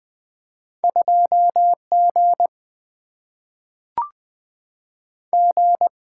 Morse code, heard as a keyed tone, about 20 wpm, 700 Hz. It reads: 2 G G
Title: Single-Letter Single-Number 20wpm